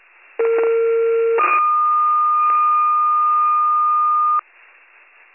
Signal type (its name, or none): two-tone page